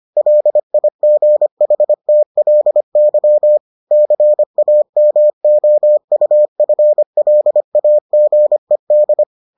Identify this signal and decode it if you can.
Morse code, heard as a keyed tone, about 25 wpm, 600 Hz. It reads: LIGHTLY CAMOUFLAGED